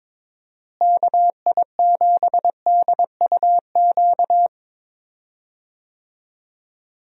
Morse code, heard as a keyed tone, about 23 wpm, 700 Hz. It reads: KI7DUQ
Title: USA Call Signs - 6 Characters - Repeated 3x - 22wpm